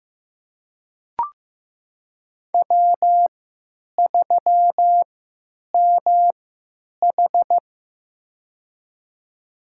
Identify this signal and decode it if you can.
Morse code, heard as a keyed tone, about 15 wpm, spaced as for 9 wpm, 700 Hz. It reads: W3MH
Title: USA Call Signs - 4 Characters - 3x Character Spacing - 15wpm